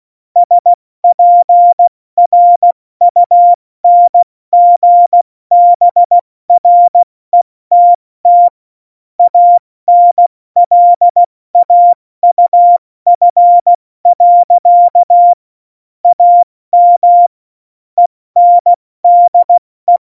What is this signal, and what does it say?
Morse code, heard as a keyed tone, about 16 wpm, 700 Hz. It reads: SPRUNGBRETT ANLAUF. AM ENDE
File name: Morsekurs / Witze / Witz 17. November 2020, 70 F80